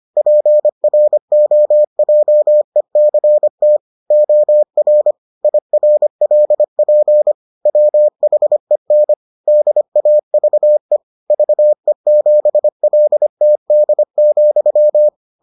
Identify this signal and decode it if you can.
Morse code, heard as a keyed tone, about 25 wpm, 600 Hz. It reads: PROJECT OR IRLP WHEN DAVE VE7LTD,